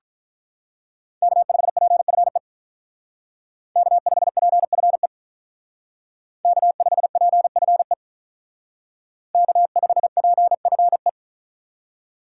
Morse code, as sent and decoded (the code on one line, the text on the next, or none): -.- ..... .--. ..-. / -.- ..... .--. ..-. / -.- ..... .--. ..-. / -.- ..... .--. ..-. .
K5PFE K5PFE K5PFE K5PFE